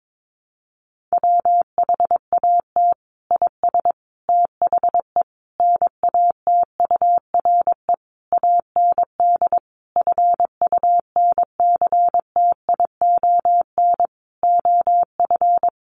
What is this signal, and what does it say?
Morse code, heard as a keyed tone, about 22 wpm, 700 Hz. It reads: WHAT IS THE NATURE AND FUNCTION OF
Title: Sentences from Top 1100 Words 22wpm